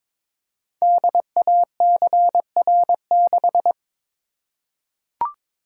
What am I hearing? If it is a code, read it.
Morse code, heard as a keyed tone, about 22 wpm, 700 Hz. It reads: DACR6
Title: Code Groups - 5 Letter:Numbers - 22wpm